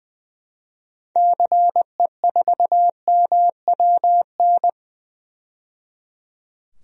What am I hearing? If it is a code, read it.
Morse code, heard as a keyed tone, about 20 wpm, 700 Hz. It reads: CE4MWN
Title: DXCC-100 Call Signs - 6 Characters - 20wpm